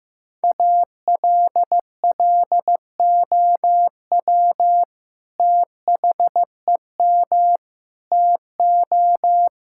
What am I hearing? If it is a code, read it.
Morse code, heard as a keyed tone, about 15 wpm, 700 Hz. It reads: ALLOW THEM TO
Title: Sentences from Top 1000 Words - Review Part 5 - 15wpm